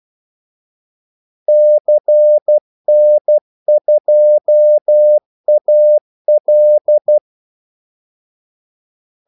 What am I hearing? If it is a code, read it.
Morse code, heard as a keyed tone, about 12 wpm, 600 Hz. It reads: CN2AL
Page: 2 ekstra ordmellemrum mellem kaldesignalerne